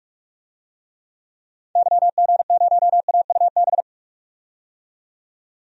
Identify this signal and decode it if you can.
Morse code, heard as a keyed tone, about 42 wpm, 700 Hz. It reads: YG0AUB
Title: DXCC-100 Call Signs - 6 Characters - 45wpm